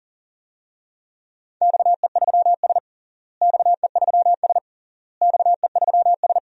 Morse code, heard as a keyed tone, about 39 wpm, 700 Hz. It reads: XE3S XE3S XE3S
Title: DXCC-100 Call Signs - 4 Characters - Repeated 3x - 40wpm